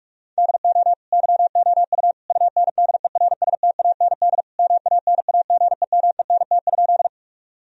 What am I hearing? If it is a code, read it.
Morse code, heard as a keyed tone, about 46 wpm, 700 Hz. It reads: DO YOU UNDERSTAND MANAGEMENT?
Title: Sentences from Top 1000 Words - Review Part 4 - 45wpm